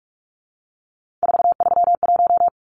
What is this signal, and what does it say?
Morse code, heard as a keyed tone, about 44 wpm, 700 Hz. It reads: 431
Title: Numbers - 3-Digit 45wpm